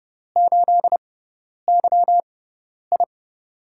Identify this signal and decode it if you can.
Morse code, heard as a keyed tone, about 30 wpm, spaced as for 10 wpm, 700 Hz. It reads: 8YI
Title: USA Call Signs - 6x Character Spacing - 30wpm